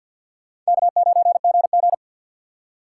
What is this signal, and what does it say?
Morse code, heard as a keyed tone, about 64 wpm, 700 Hz. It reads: K9GG